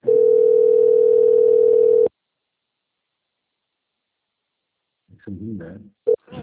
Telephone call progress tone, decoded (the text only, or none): ringback tone